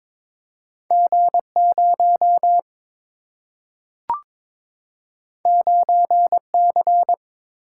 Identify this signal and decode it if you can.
Morse code, heard as a keyed tone, about 22 wpm, 700 Hz. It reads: G 0 9 C
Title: Single-Letter Single-Number 22wpm